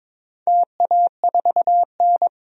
Morse code, heard as a keyed tone, about 22 wpm, 700 Hz. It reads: TA4N
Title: DXCC-100 Call Signs - 4 Characters - 22wpm